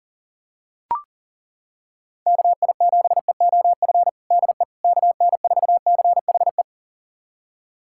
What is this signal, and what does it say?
Morse code, heard as a keyed tone, about 40 wpm, 700 Hz. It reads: KI7EOF DE KN4CHE
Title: USA Callsign DE Callsign - 6 Characters - 40wpm